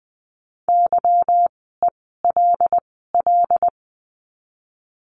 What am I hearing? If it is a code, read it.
Morse code, heard as a keyed tone, about 20 wpm, 700 Hz. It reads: Y E L L
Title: 146 Lesson 24f - Letter y - Words 20wpm